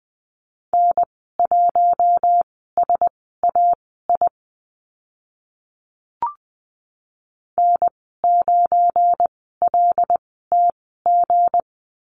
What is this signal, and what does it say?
Morse code, heard as a keyed tone, about 20 wpm, spaced as for 14 wpm, 700 Hz. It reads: N1SAI N9LTG
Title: 179 Lesson 29g - Slash - Call Signs 20wpm